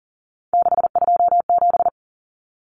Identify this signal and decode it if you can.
Morse code, heard as a keyed tone, about 40 wpm, 700 Hz. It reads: 627
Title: Numbers - 3-Digit 40wpm